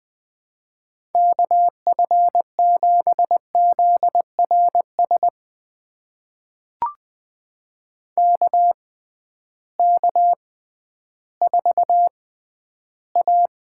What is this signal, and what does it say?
Morse code, heard as a keyed tone, about 19 wpm, spaced as for 12 wpm, 700 Hz. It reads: KF7ZRS KK4A
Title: USA Call Signs - 6 Characters - 6x Character Spacing - 20wpm